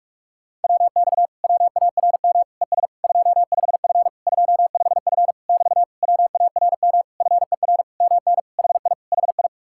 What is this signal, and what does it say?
Morse code, heard as a keyed tone, about 49 wpm, 700 Hz. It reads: WX WARM ES 25F 25F <BT> WARM FER MN HI HI